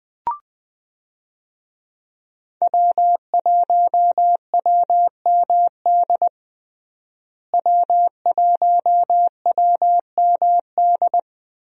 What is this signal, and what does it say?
Morse code, heard as a keyed tone, about 20 wpm, 700 Hz. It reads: W1WMD W1WMD
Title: USA Call Signs - 5 Characters - Repeated 3x - 20wpm